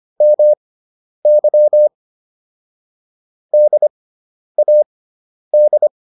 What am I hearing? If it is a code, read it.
Morse code, heard as a keyed tone, about 25 wpm, spaced as for 10 wpm, 600 Hz. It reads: MY DAD